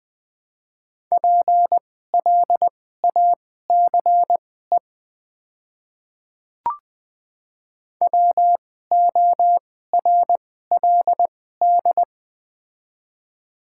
Morse code, heard as a keyed tone, about 20 wpm, spaced as for 14 wpm, 700 Hz. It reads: PLACE WORLD
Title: 2x Character Spacing - Top 200 Words - 20wpm